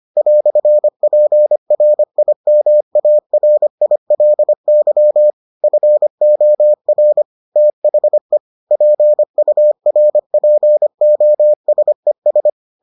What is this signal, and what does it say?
Morse code, heard as a keyed tone, about 25 wpm, 600 Hz. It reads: "PRIMARILY FOR THE PURPOSES